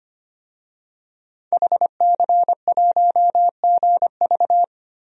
Morse code, heard as a keyed tone, about 25 wpm, 700 Hz. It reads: HC1GV
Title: DXCC-100 Call Signs - 5 Characters - 25wpm